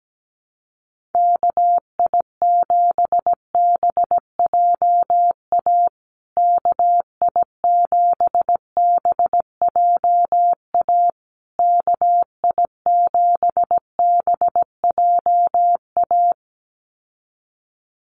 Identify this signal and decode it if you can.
Morse code, heard as a keyed tone, about 17 wpm, 700 Hz. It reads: KI7BJA KI7BJA KI7BJA
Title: USA Call Signs Repeated 3x 17wpm